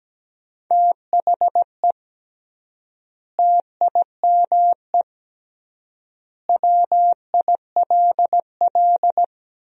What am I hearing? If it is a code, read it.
Morse code, heard as a keyed tone, about 17 wpm, 700 Hz. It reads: THE TIME WILL
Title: 3x Word-Spacing - Sentences from Top 100 Words - 17wpm